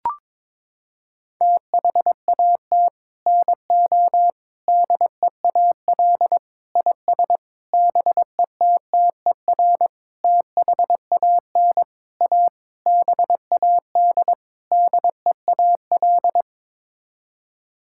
Morse code, heard as a keyed tone, about 22 wpm, 700 Hz. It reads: THAT NO DEAL IS BETTER THAN A BAD DEAL
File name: Sets of 9 Words - 22wpm